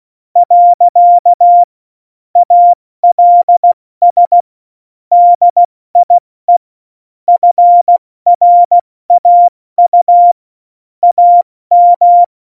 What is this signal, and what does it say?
Morse code, heard as a keyed tone, about 16 wpm, spaced as for 14 wpm, 700 Hz. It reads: . ALS DIE FRAU AM
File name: Morsekurs / Witze / Witz 23. Oktober 2018, 70 F80